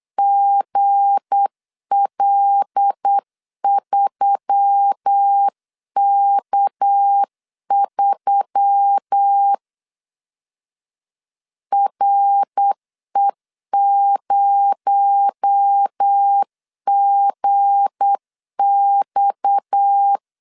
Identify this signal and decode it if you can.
Morse code, heard as a keyed tone, about 8 wpm, 800 Hz. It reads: GL3K3 RE0GX
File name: LEZIONI / 7 GF9Z / 7 G 8wpm 3118